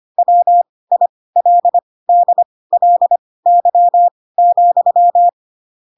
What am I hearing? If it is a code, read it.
Morse code, heard as a keyed tone, about 25 wpm, spaced as for 18 wpm, 700 Hz. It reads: WILDLY,
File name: Pinocchio / Pinocchio ch6 12 25 ik1hge